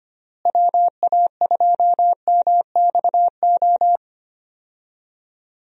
Morse code, heard as a keyed tone, about 25 wpm, 700 Hz. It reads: WA2MXO
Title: USA Call Signs - 6 Characters - Repeated 2x - 25wpm